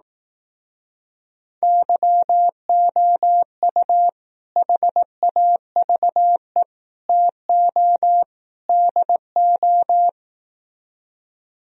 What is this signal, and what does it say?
Morse code, heard as a keyed tone, about 18 wpm, 700 Hz. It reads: YOU HAVE TO DO
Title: Speed-Racing - Sets of 4 Words 15wpm